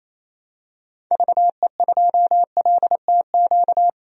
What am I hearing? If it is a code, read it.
Morse code, heard as a keyed tone, about 28 wpm, 700 Hz. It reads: VE2LTQ
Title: DXCC-100 Call Signs - 6 Characters - 28wpm